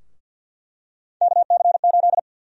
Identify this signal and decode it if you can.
Morse code, heard as a keyed tone, about 50 wpm, 700 Hz. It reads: KC8
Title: USA Call Sign Prefixes - Repeated 2x - 50wpm